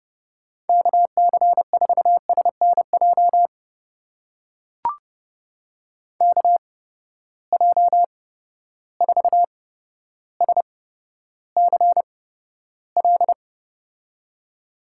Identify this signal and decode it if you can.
Morse code, heard as a keyed tone, about 32 wpm, spaced as for 13 wpm, 700 Hz. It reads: KC4SNJ KJ4SCL
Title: USA Call Signs - 6 Characters - 8x Character Spacing - 30wpm